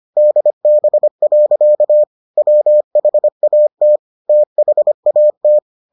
Morse code, heard as a keyed tone, about 25 wpm, 600 Hz. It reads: DB. WHAT THAT